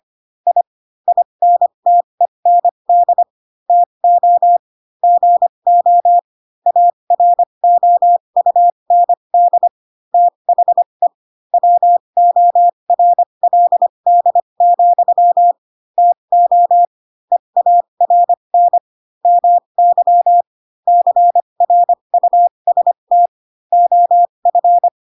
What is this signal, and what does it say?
Morse code, heard as a keyed tone, about 25 wpm, spaced as for 22 wpm, 700 Hz. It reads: I INTEND TO GO AROUND THE WORLD, TO EARN MY CRUST OF
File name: Pinocchio / Pinocchio ch2 18 25 ik1hge